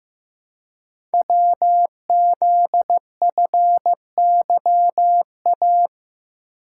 Morse code, heard as a keyed tone, about 15 wpm, 700 Hz. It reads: WZFYA
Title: Code Groups - 5 Letters - 15wpm